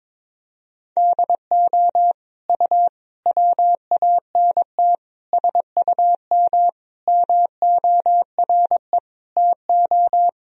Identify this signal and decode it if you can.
Morse code, heard as a keyed tone, about 22 wpm, 700 Hz. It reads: DO U WANT SUM MORE TO
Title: Sentences from Top 300 Words and Abbreviations 22wpm